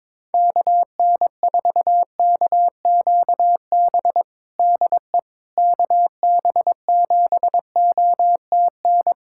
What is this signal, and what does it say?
Morse code, heard as a keyed tone, about 22 wpm, 700 Hz. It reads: KN4KQB DE KB7OTN
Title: USA Callsign DE Callsign 22wpm